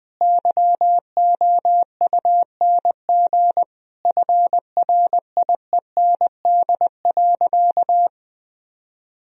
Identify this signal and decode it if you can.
Morse code, heard as a keyed tone, about 21 wpm, 700 Hz. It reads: YOUNG FRIEND.
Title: Sentences from Top 1000 Words - Review Part 3 - 20wpm